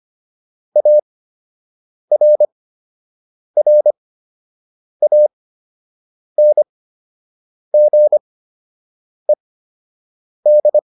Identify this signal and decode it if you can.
Morse code, heard as a keyed tone, about 25 wpm, spaced as for 7 wpm, 600 Hz. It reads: ARRANGED